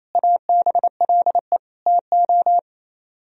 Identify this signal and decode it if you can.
Morse code, heard as a keyed tone, about 38 wpm, 700 Hz. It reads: ABLE TO